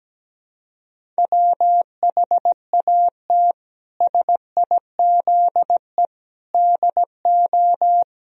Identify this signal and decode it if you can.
Morse code, heard as a keyed tone, about 17 wpm, 700 Hz. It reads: WHAT SIZE DO
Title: Sentences from Top 1000 Words - Review Part 2 - 17wpm